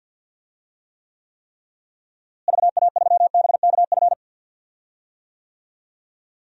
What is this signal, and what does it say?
Morse code, heard as a keyed tone, about 46 wpm, 700 Hz. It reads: VA3BKF
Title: DXCC-100 Call Signs - 6 Characters - Repeated 3x - 50wpm